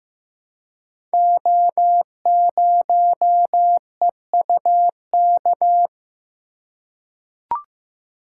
Morse code, heard as a keyed tone, about 15 wpm, 700 Hz. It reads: O0EUK
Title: Code Groups - 5 Letter:Numbers - 15wpm